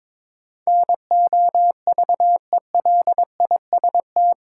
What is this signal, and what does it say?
Morse code, heard as a keyed tone, about 22 wpm, 700 Hz. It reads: NOVELIST
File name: practice / 22wpm / Eight Letter Words - 22wpm